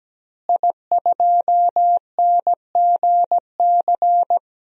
Morse code, heard as a keyed tone, about 17 wpm, 700 Hz. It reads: I2NGC
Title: DXCC-100 Call Signs - 5 Characters - Repeated 3x - 17wpm